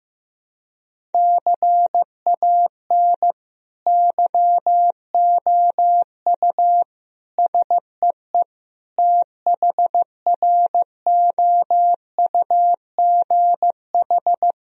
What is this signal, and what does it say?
Morse code, heard as a keyed tone, about 15 wpm, 700 Hz. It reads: CAN YOU SEE THROUGH
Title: Sentences from Top 1000 Words - Review Part 1 - 15wpm